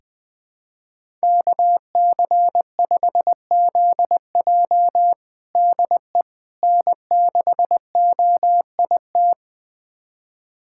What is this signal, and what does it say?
Morse code, heard as a keyed tone, about 21 wpm, 700 Hz. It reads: KC5ZJ DE N6OIT
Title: USA Callsign DE Callsign - 5 Characters - 20wpm